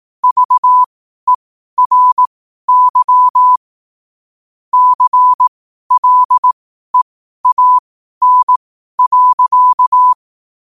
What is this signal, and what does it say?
Morse code, heard as a keyed tone, about 18 wpm, spaced as for 12 wpm, 1000 Hz. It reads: VERY CLEAN.
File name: qotdfiles / 12wpm